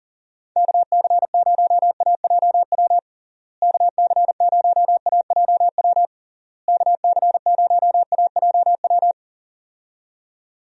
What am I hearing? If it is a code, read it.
Morse code, heard as a keyed tone, about 40 wpm, 700 Hz. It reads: KC0AJW KC0AJW KC0AJW
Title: USA Call Signs - 6 Characters - Repeated 3x - 40wpm